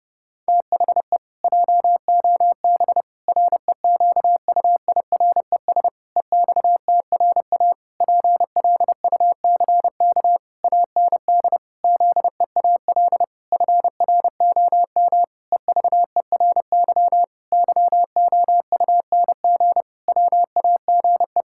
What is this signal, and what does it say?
Morse code, heard as a keyed tone, about 30 wpm, 700 Hz. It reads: THE JOB REQUIRES EXTRA PLUCK AND ZEAL FROM EVERY YOUNG WAGE
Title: Pangram Challenge v2 - 30wpm